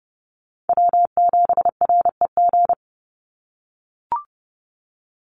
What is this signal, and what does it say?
Morse code, heard as a keyed tone, about 30 wpm, 700 Hz. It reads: W7REG
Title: USA Call Signs 30wpm